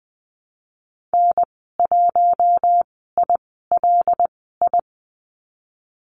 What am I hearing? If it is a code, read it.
Morse code, heard as a keyed tone, about 19 wpm, spaced as for 14 wpm, 700 Hz. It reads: N1ILI
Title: 088 Lesson 15g - Number 5 - Call Signs 20wpm